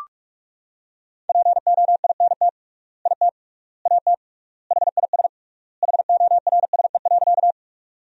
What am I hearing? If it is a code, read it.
Morse code, heard as a keyed tone, about 46 wpm, 700 Hz. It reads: POINT IT AT HIS HORSE.
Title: 3x Word-Spacing - Sentences from Top 300 Words - 45wpm